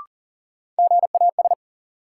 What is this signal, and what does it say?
Morse code, heard as a keyed tone, about 37 wpm, 700 Hz. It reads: GAS
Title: Sentences from Top 1100 Words - New Word Drill - 40wpm